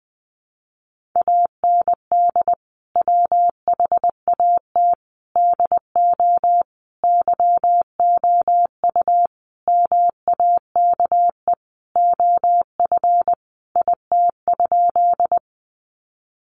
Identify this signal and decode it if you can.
Morse code, heard as a keyed tone, about 20 wpm, 700 Hz. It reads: AND WHAT DO YOU MAKE OF IT?
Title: Sentences from Top 100 Words 20wpm